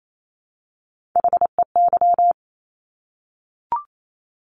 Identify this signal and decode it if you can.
Morse code, heard as a keyed tone, about 28 wpm, 700 Hz. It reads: HEY
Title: Top Three Letter Combinations in Words 28wpm